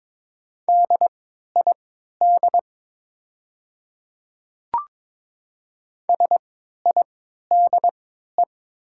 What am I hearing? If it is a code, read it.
Morse code, heard as a keyed tone, about 22 wpm, spaced as for 13 wpm, 700 Hz. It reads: DID SIDE
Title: 3x Character Spacing - Top 100 Words - 22wpm